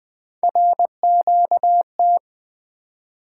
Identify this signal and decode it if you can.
Morse code, heard as a keyed tone, about 20 wpm, 700 Hz. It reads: RQT
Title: Three Letter-Number Combinations - 20wpm